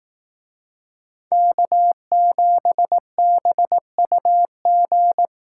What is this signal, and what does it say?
Morse code, heard as a keyed tone, about 20 wpm, 700 Hz. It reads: K7BUG